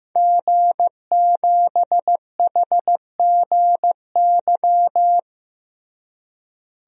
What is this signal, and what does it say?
Morse code, heard as a keyed tone, about 14 wpm, 700 Hz. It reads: G7HGY
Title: DXCC-100 Call Signs - 5 Characters - Repeated 2x - 15wpm